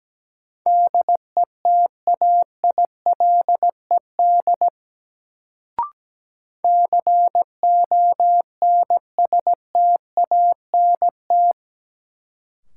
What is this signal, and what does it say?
Morse code, heard as a keyed tone, about 17 wpm, 700 Hz. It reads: DETAILED CONSTANT
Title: Sentences from Top 1500 Words - New Word Drill - 17wpm